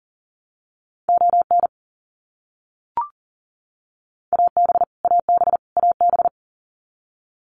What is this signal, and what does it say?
Morse code, heard as a keyed tone, about 40 wpm, 700 Hz. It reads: ON AB AB AB
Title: Canadian Provinces and Territories - Repeated 3x - 40wpm